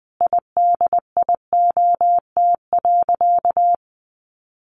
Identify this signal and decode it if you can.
Morse code, heard as a keyed tone, about 20 wpm, 700 Hz. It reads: IDIOT.
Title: A Sound Of Thunder 20wpm